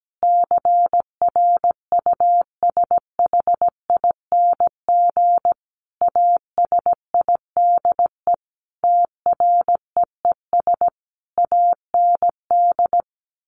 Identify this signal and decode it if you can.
Morse code, heard as a keyed tone, about 17 wpm, 700 Hz. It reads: CRUSHING ASIDE TREES AND
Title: A Sound Of Thunder 17wpm